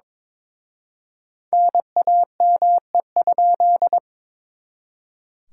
Morse code, heard as a keyed tone, about 26 wpm, 700 Hz. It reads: NAME?
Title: Speed-Racing Common QSO Phrases - 22wpm